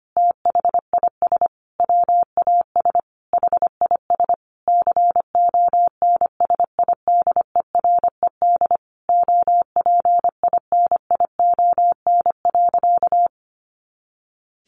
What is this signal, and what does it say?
Morse code, heard as a keyed tone, about 25 wpm, 700 Hz. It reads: THIS WAS HIS CONSIDERED OPINION.
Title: A Sound Of Thunder 25wpm